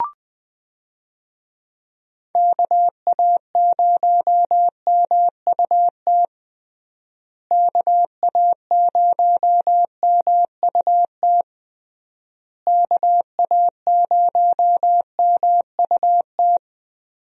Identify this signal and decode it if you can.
Morse code, heard as a keyed tone, about 20 wpm, 700 Hz. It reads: KA0MUT KA0MUT KA0MUT
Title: USA Call Signs - 6 Characters - Repeated 3x - 20wpm